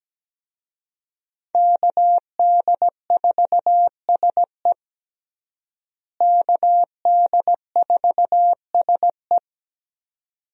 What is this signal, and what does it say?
Morse code, heard as a keyed tone, about 17 wpm, 700 Hz. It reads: KD4SE KD4SE
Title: USA Call Signs - 5 Characters - Repeated 2x - 17wpm